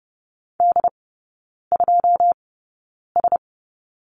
Morse code, heard as a keyed tone, about 30 wpm, spaced as for 9 wpm, 690 Hz. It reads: D2S
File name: Lesson 17e - Letter u - Three Characters 30wpm